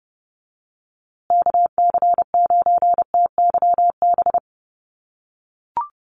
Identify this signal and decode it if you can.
Morse code, heard as a keyed tone, about 30 wpm, 700 Hz. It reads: KC9TYB
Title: USA Call Signs Repeated 3x 30wpm